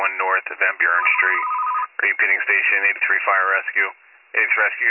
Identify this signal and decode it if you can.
Emergency Alert System (SAME) tones, heard as harsh buzzing bursts, with speech. It is heard from 0.4 to 3.9 s.